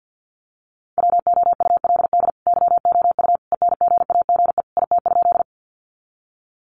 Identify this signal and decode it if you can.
Morse code, heard as a keyed tone, about 50 wpm, 700 Hz. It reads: WOULD YOU ENGAGE IT?
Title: Sentences from Top 1500 Words 50wpm